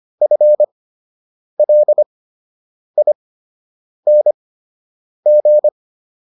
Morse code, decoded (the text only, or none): FLING